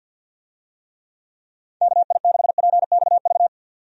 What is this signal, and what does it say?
Morse code, heard as a keyed tone, about 51 wpm, 700 Hz. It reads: KI6PXV
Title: USA Call Signs - 6 Characters - 50wpm